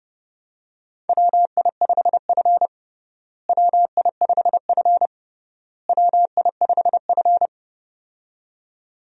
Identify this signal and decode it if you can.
Morse code, heard as a keyed tone, about 30 wpm, 700 Hz. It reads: WI5F WI5F WI5F
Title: USA Call Signs - 4 Characters - Repeated 3x - 30wpm